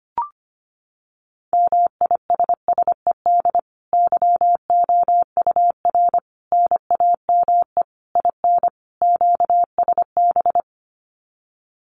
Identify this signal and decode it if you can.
Morse code, heard as a keyed tone, about 25 wpm, 700 Hz. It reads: MISSED YOUR NAME IN QSB
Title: Common QSO Phrases 25wpm